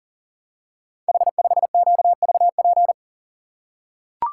Morse code, heard as a keyed tone, about 38 wpm, 700 Hz. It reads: H5QVP
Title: Code Groups - 5 Letter:Numbers - 40wpm